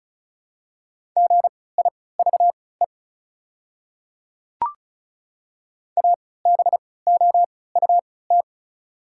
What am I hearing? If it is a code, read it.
Morse code, heard as a keyed tone, about 36 wpm, spaced as for 20 wpm, 690 Hz. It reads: GIVE ABOUT
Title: 3x Character Spacing - Top 300 Words - 35wpm